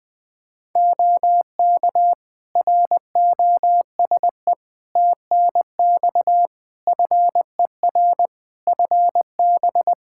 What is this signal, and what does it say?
Morse code, heard as a keyed tone, about 20 wpm, 700 Hz. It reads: OK ROSE TNX FER FB
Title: Practice QSO - Exchange 4 - Ending - 20wpm